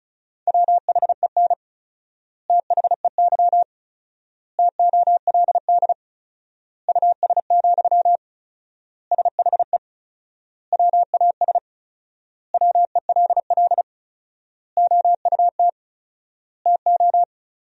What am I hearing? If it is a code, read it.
Morse code, heard as a keyed tone, about 34 wpm, 700 Hz. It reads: WHEN THEY TOLD US, SHE WAS WELL OUT TO
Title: 4x Word-Spacing - Sentences from Top 400 Words - 35wpm